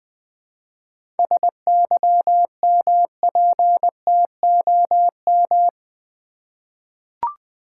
Morse code, heard as a keyed tone, about 20 wpm, 690 Hz. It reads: SYMPTOM
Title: 7 Letter Words - First 3 Letters Given - 20wpm